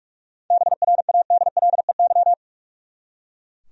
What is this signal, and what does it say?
Morse code, heard as a keyed tone, about 43 wpm, 700 Hz. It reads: BRADLEY
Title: USA Common Names - 45wpm